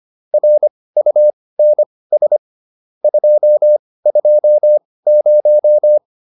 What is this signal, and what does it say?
Morse code, heard as a keyed tone, about 25 wpm, spaced as for 18 wpm, 600 Hz. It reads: RUNS 220